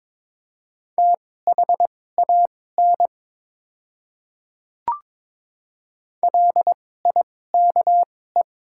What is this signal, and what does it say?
Morse code, heard as a keyed tone, about 21 wpm, spaced as for 16 wpm, 700 Hz. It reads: THAN LIKE
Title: 2x Character Spacing - Top 100 Words - 22wpm